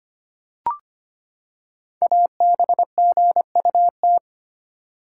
Morse code, heard as a keyed tone, about 25 wpm, 700 Hz. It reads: ABGUT
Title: Code Groups - 5 Letters - 25wpm